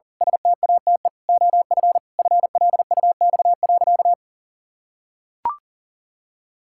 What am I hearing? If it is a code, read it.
Morse code, heard as a keyed tone, about 40 wpm, 700 Hz. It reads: STATE OF FLUX.
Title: Pangram Challenge v2 - 40wpm